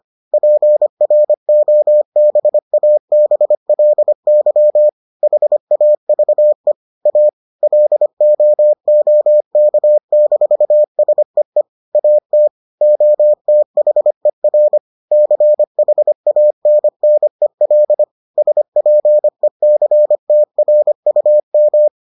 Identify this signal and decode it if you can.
Morse code, heard as a keyed tone, about 25 wpm, 600 Hz. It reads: PROBABLY HAVE A LOOK-SEE AT OTHER CHANNEL SPECTRUM